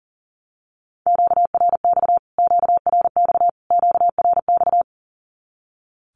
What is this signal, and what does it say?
Morse code, heard as a keyed tone, about 40 wpm, 700 Hz. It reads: QRX QRX QRX
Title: Q-Codes Repeated 3x 40wpm